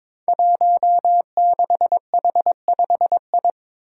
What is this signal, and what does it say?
Morse code, heard as a keyed tone, about 22 wpm, 700 Hz. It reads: 16H5I
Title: Five Challenging Letter-Number Combinations 22wpm